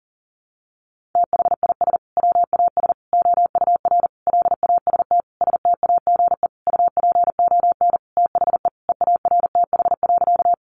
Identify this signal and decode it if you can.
Morse code, heard as a keyed tone, about 40 wpm, 700 Hz. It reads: THIS WAS OUR LAST STAGE UPON THE EARTH.
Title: Sentences from Top 1200 Words 40wpm